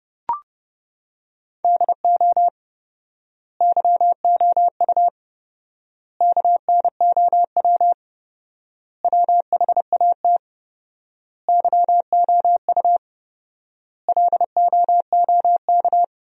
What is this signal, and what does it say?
Morse code, heard as a keyed tone, about 32 wpm, 700 Hz. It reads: DO YOU KNOW WHAT YOU LOOK
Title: 4x Word-Spacing - Sentences from Top 100 Words - 30wpm